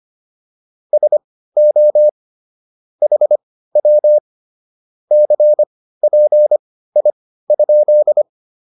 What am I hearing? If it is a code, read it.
Morse code, heard as a keyed tone, about 25 wpm, spaced as for 15 wpm, 600 Hz. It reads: SO HW CPI?